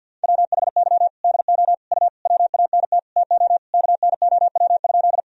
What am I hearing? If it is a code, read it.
Morse code, heard as a keyed tone, about 50 wpm, 700 Hz. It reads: WHY DO U WANT TO KNOW?
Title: Sentences from Top 200 Words and Abbreviations 50wpm